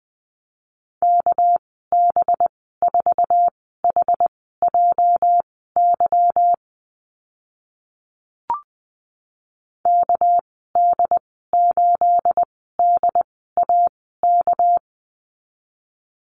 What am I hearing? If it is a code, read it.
Morse code, heard as a keyed tone, about 20 wpm, spaced as for 14 wpm, 700 Hz. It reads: KB4HJY KD8DAK
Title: 231 Lesson 37g - Letter x - Call Signs 20wpm